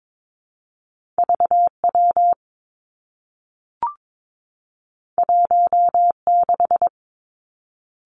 Morse code, heard as a keyed tone, about 22 wpm, 700 Hz. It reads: V W 1 6
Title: Two Challenging Letter-Number Combinations 22wpm